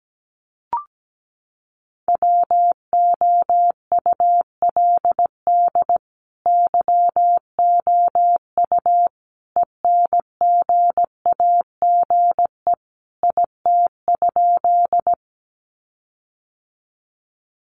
Morse code, heard as a keyed tone, about 17 wpm, 700 Hz. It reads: WOULD YOU ENGAGE IT?
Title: Sentences from Top 1500 Words 17wpm